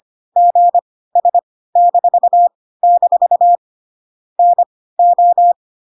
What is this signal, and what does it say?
Morse code, heard as a keyed tone, about 25 wpm, spaced as for 16 wpm, 700 Hz. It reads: GS-- NO
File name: Pinocchio / Pinocchio ch1 10 25 ik1hge